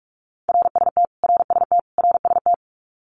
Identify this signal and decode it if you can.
Morse code, heard as a keyed tone, about 44 wpm, 700 Hz. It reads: RST RST RST
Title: Abbreviations Repeated 3x - 45wpm